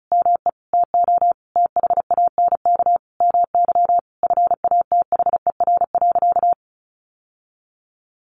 Morse code, heard as a keyed tone, about 34 wpm, 700 Hz. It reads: ME TO THANK MY FATHER.
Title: Sentences from Top 1100 Words 35wpm